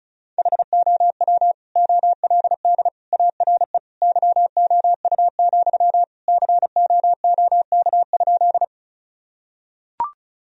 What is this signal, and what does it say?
Morse code, heard as a keyed tone, about 36 wpm, 700 Hz. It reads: HOW OLD ARE YOU, COOK?
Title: Sentences from Top 1000 Words - Review Part 4 - 35wpm